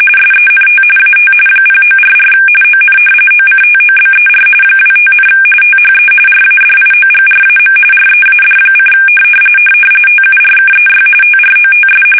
RTTY, about 45.45 baud. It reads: SENT AT 0944 ON OCTOBER 25, THE RADIOGRAM READ: TURKEY TROTS
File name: fsk850 / battleshipnj